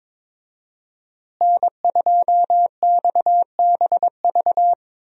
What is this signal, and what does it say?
Morse code, heard as a keyed tone, about 22 wpm, 700 Hz. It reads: N2XBV
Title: USA Call Signs - 5 Characters - Repeated 3x - 22wpm